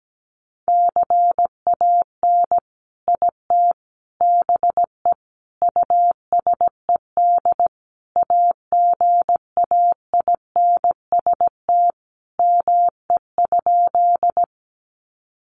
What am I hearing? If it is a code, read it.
Morse code, heard as a keyed tone, about 17 wpm, 700 Hz. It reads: CAN IT BE USED AGAINST ME?
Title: Sentences from Top 600 Words 17wpm